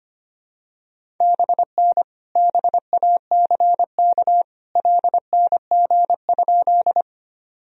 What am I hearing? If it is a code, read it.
Morse code, heard as a keyed tone, about 25 wpm, 700 Hz. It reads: BN BACK LNG?
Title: Sentences from Top 200 Words and Abbreviations 25wpm